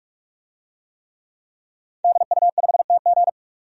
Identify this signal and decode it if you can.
Morse code, heard as a keyed tone, about 48 wpm, 700 Hz. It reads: DU5TG
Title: DXCC-100 Call Signs - 5 Characters - 45wpm